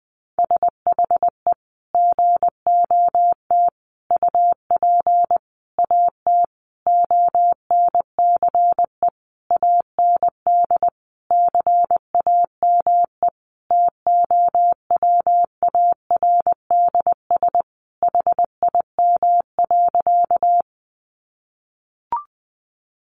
Morse code, heard as a keyed tone, about 20 wpm, 700 Hz. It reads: SHE GOT UP AT ONCE AND CAME TOWARDS HIM.
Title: Sentences from Top 1100 Words 20wpm